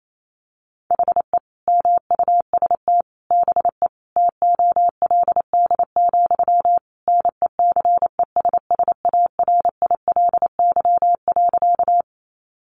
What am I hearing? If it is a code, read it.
Morse code, heard as a keyed tone, about 28 wpm, 700 Hz. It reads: HE MUST BE TOLD, NECESSARILY.
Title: Sentences from Top 1400 Words 28wpm